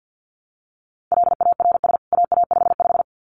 Morse code, heard as a keyed tone, about 50 wpm, 700 Hz. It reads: LARS AA55
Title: CWT Exchange 50wpm